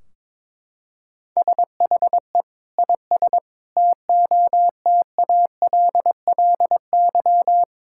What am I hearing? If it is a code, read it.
Morse code, heard as a keyed tone, about 22 wpm, 700 Hz. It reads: SHE IS TOTALLY
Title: Sentences from Top 1600 Words 22wpm